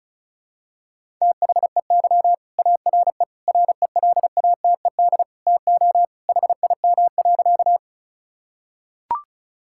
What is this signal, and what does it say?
Morse code, heard as a keyed tone, about 36 wpm, 690 Hz. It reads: THEY ARE RELATED TO HIM.
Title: Sentences from Top 1000 Words - Review Part 5 - 35wpm